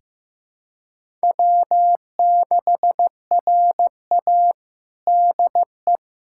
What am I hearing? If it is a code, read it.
Morse code, heard as a keyed tone, about 16 wpm, 700 Hz. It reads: W6RA DE
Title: USA Callsign DE Callsign - 4 Characters - 15wpm